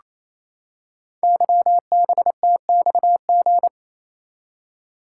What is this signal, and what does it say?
Morse code, heard as a keyed tone, about 29 wpm, 700 Hz. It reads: YBTXG
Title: Code Groups - 5 Letters - 28wpm